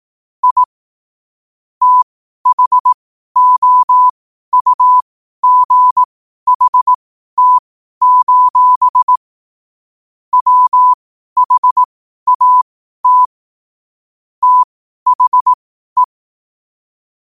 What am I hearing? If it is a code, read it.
Morse code, heard as a keyed tone, about 18 wpm, spaced as for 12 wpm, 1000 Hz. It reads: I THOUGHT: WHAT THE
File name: qotdfiles / 12wpm